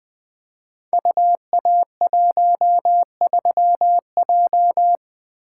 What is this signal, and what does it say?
Morse code, heard as a keyed tone, about 20 wpm, 700 Hz. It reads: UA13J
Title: Code Groups - 5 Letter:Numbers - 20wpm